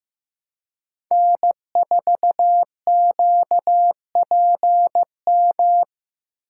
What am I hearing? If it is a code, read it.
Morse code, heard as a keyed tone, about 18 wpm, 700 Hz. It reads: N4QPM
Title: Speed Racing - USA 1x3 Call Signs 15wpm